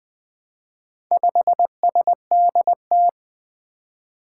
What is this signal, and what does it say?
Morse code, heard as a keyed tone, about 20 wpm, 700 Hz. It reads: 5SDT
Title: Four Challenging Letter-Number Combinations 20wpm